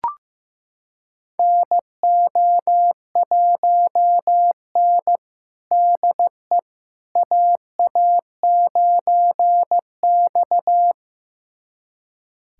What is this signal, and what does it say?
Morse code, heard as a keyed tone, about 15 wpm, 700 Hz. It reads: NO1N DE AA9X
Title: USA Callsign DE Callsign - 4 Characters - 15wpm